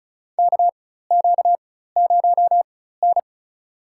Morse code, heard as a keyed tone, about 36 wpm, spaced as for 16 wpm, 700 Hz. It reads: KQ0N
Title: USA Call Signs - 4 Characters - 4x Character Spacing - 35wpm